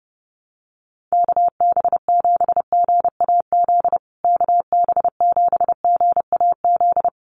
Morse code, heard as a keyed tone, about 30 wpm, 700 Hz. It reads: KB7GAZ KB7GAZ
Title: USA Call Signs Repeated 2x 30wpm